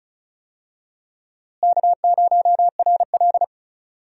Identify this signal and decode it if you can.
Morse code, heard as a keyed tone, about 34 wpm, 700 Hz. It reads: K0RL
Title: USA Call Signs - 4 Characters - Repeated 2x - 35wpm